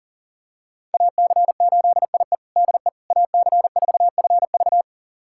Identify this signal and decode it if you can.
Morse code, heard as a keyed tone, about 37 wpm, 690 Hz. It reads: AC8IE DE AC4FV
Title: USA Callsign DE Callsign - 5 Characters - 40wpm